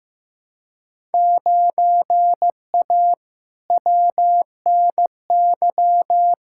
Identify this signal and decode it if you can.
Morse code, heard as a keyed tone, about 15 wpm, 700 Hz. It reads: 9A WNY
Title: Field Day Exchanges Encore - 15wpm